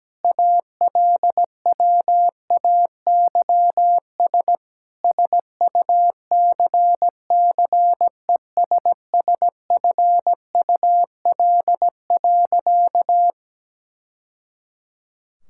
Morse code, heard as a keyed tone, about 17 wpm, 700 Hz. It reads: ALWAYS SUCCESSFUL.
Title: Sentences from Top 1000 Words - Review Part 6 - 17wpm